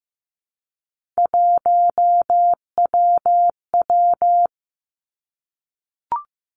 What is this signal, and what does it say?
Morse code, heard as a keyed tone, about 15 wpm, 700 Hz. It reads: 1WW
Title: Three Challenging Letter-Number Combinations 15wpm